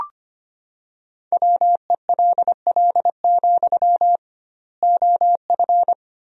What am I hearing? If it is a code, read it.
Morse code, heard as a keyed tone, about 25 wpm, 700 Hz. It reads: WELL, OF
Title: 2x Word-Spacing - Sentences from Top 500 Words - 25wpm